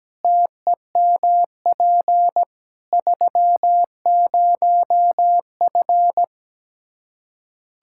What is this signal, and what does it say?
Morse code, heard as a keyed tone, about 17 wpm, 700 Hz. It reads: TEMP 30F
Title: K1USN SST QSO Practice 17wpm